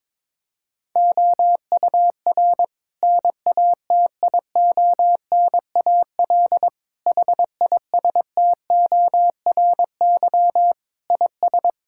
Morse code, heard as a keyed tone, about 22 wpm, 700 Hz. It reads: OUR NATIONAL HISTORY IS
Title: Sentences from Top 1000 Words - Review Part 6 - 22wpm